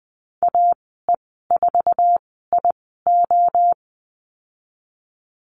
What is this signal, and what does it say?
Morse code, heard as a keyed tone, about 20 wpm, spaced as for 14 wpm, 700 Hz. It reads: AE4IO
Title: 082 Lesson 14g - Number 2 - Call Signs 20wpm